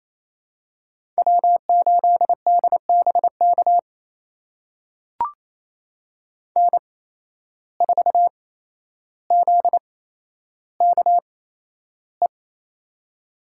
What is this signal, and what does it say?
Morse code, heard as a keyed tone, about 28 wpm, spaced as for 12 wpm, 700 Hz. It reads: W8DBK N4ZKE
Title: USA Call Signs - 5 Characters - 8x Character Spacing - 28wpm